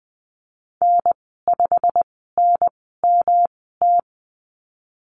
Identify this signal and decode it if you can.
Morse code, heard as a keyed tone, about 19 wpm, spaced as for 14 wpm, 700 Hz. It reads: N5NMT
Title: 147 Lesson 24g - Letter y - Call Signs 20wpm